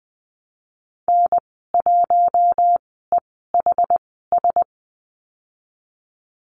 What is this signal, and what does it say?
Morse code, heard as a keyed tone, about 20 wpm, spaced as for 15 wpm, 700 Hz. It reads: N1EHS
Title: 193 Lesson 31g - Letter v - Call Signs 20wpm